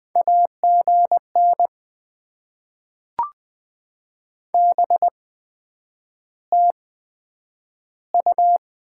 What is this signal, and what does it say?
Morse code, heard as a keyed tone, about 20 wpm, spaced as for 9 wpm, 700 Hz. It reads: AGN BTU
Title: Top 20 QSO Elements - 8x Character Spacing - 20wpm